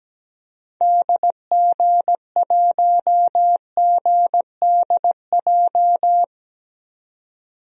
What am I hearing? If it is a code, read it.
Morse code, heard as a keyed tone, about 17 wpm, 700 Hz. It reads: DG1GDJ
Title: DXCC-100 Call Signs - 6 Characters - 17wpm